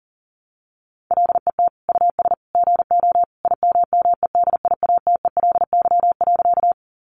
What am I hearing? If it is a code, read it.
Morse code, heard as a keyed tone, about 40 wpm, 700 Hz. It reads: LET US GO IMMEDIATELY.
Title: Sentences from Top 1100 Words 40wpm